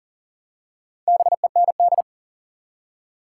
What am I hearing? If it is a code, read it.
Morse code, heard as a keyed tone, about 40 wpm, 700 Hz. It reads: BEND
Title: Four Letter Words - 40wpm